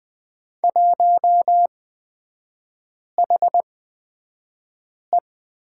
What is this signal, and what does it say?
Morse code, heard as a keyed tone, about 20 wpm, spaced as for 5 wpm, 700 Hz. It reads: 1HE
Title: 250 Lesson 40e - BK - Three Characters 20wpm